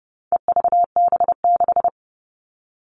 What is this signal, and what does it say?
Morse code, heard as a keyed tone, about 30 wpm, 700 Hz. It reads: EVB6
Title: All Cut Numbers - 4-Digit 30wpm